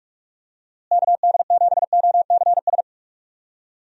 Morse code, heard as a keyed tone, about 46 wpm, 700 Hz. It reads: KD7OCS
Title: USA Call Signs - 6 Characters - Repeated 3x - 45wpm